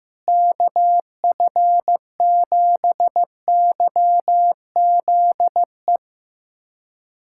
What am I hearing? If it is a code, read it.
Morse code, heard as a keyed tone, about 15 wpm, 700 Hz. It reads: KF7YZE
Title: USA Callsign DE Callsign 15wpm